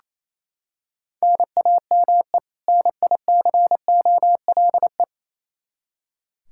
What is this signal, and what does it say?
Morse code, heard as a keyed tone, about 28 wpm, 700 Hz. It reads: NAME NICOLE
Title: Practice QSO - Exchange 1 - Beginning - 28wpm